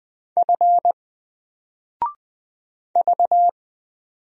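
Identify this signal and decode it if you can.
Morse code, heard as a keyed tone, about 20 wpm, 700 Hz. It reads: F V
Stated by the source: Single letter and number Morse drill at 20 wpm, from N4LCD.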